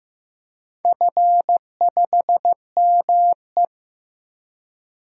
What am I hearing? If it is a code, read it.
Morse code, heard as a keyed tone, about 15 wpm, 700 Hz. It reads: F5ME
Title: DXCC-100 Call Signs - 4 Characters - 15wpm